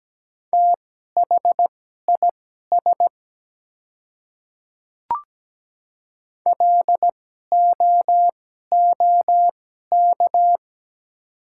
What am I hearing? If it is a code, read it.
Morse code, heard as a keyed tone, about 17 wpm, spaced as for 12 wpm, 700 Hz. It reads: THIS LOOK
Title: 2x Character Spacing - Top 100 Words - 17wpm